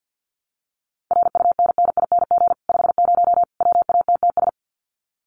Morse code, heard as a keyed tone, about 50 wpm, 700 Hz. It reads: RUNNING 50 WATTS
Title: Common QSO Phrases 50wpm